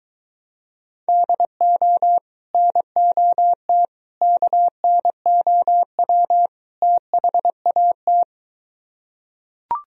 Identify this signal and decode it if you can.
Morse code, heard as a keyed tone, about 23 wpm, 700 Hz. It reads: DO NOT KNOW THAT